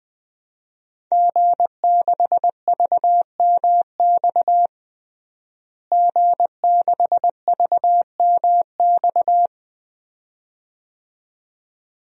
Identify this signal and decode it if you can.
Morse code, heard as a keyed tone, about 20 wpm, 700 Hz. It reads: G6VMX G6VMX
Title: DXCC-100 Call Signs - 5 Characters - Repeated 2x - 20wpm